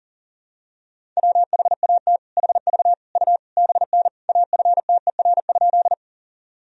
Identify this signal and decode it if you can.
Morse code, heard as a keyed tone, about 40 wpm, 700 Hz. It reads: WHAT HV U BN AFTER?
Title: Sentences from Top 200 Words and Abbreviations 40wpm